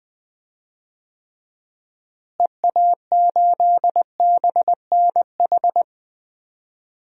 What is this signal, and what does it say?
Morse code, heard as a keyed tone, about 20 wpm, 700 Hz. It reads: EA8BNH
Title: DXCC-100 Call Signs - 6 Characters - Repeated 3x - 20wpm